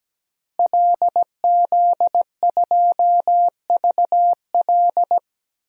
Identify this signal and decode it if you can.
Morse code, heard as a keyed tone, about 17 wpm, 700 Hz. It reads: LZ2VL
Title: DXCC-100 Call Signs - 5 Characters - Repeated 3x - 17wpm